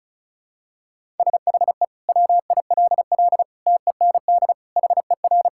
Morse code, heard as a keyed tone, about 36 wpm, 700 Hz. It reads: SHE WILL TEND HER
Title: Sentences from Top 1000 Words - Review Part 5 - 35wpm